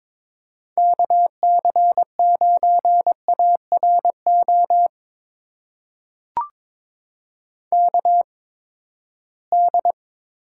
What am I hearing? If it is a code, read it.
Morse code, heard as a keyed tone, about 22 wpm, spaced as for 15 wpm, 700 Hz. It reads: KC9ARO KD
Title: USA Call Signs - 6 Characters - 8x Character Spacing - 22wpm